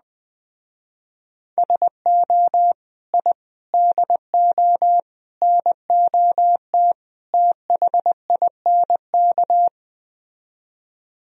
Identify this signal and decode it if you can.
Morse code, heard as a keyed tone, about 20 wpm, 700 Hz. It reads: SO I DO NOT THINK
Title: Speed-Racing - Sets of 5 Words 17wpm